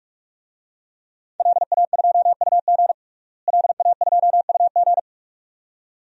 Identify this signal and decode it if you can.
Morse code, heard as a keyed tone, about 42 wpm, 700 Hz. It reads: LA2UG LA2UG
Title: DXCC-100 Call Signs - 5 Characters - Repeated 2x - 45wpm